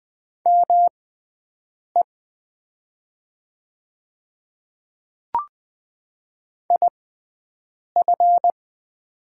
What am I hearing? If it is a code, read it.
Morse code, heard as a keyed tone, about 20 wpm, spaced as for 7 wpm, 700 Hz. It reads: ME IF
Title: 6x Character Spacing - Top 200 Words - 20wpm